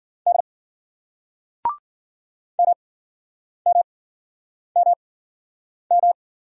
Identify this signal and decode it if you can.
Morse code, heard as a keyed tone, about 51 wpm, spaced as for 9 wpm, 700 Hz. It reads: D MMMM